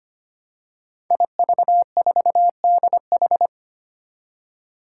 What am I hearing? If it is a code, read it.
Morse code, heard as a keyed tone, about 25 wpm, 700 Hz. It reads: IV4DH
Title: Five Challenging Letter-Number Combinations 25wpm